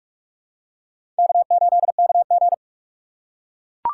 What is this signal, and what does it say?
Morse code, heard as a keyed tone, about 42 wpm, 700 Hz. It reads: K8KG
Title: USA Call Signs - 4 Characters - 6x Character Spacing - 45wpm